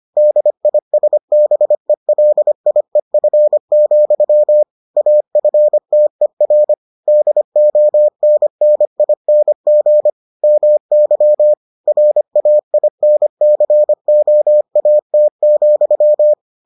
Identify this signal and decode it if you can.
Morse code, heard as a keyed tone, about 25 wpm, 600 Hz. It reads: DISBELIEF, AFTER DONNING MY RAINCOAT,